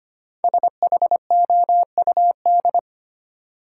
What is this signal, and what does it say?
Morse code, heard as a keyed tone, about 25 wpm, 700 Hz. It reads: SHOUD
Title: Sentences from Top 1000 Words - Review Part 4 - New Word Drill - 25wpm